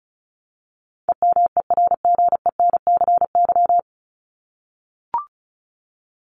Morse code, heard as a keyed tone, about 34 wpm, 700 Hz. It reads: EMERGENCY
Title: Top 1501-1750 Words 35wpm